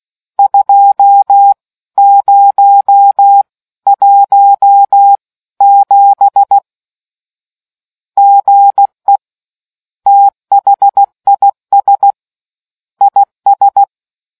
Morse code, heard as a keyed tone, about 16 wpm, spaced as for 14 wpm, 800 Hz. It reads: 2017 GE THIS IS